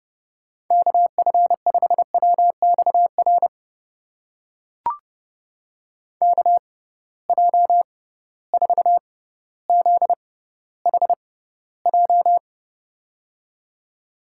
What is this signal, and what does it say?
Morse code, heard as a keyed tone, about 30 wpm, spaced as for 15 wpm, 700 Hz. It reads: KF5WXR KJ4ZHJ
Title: USA Call Signs - 6 Characters - 6x Character Spacing - 30wpm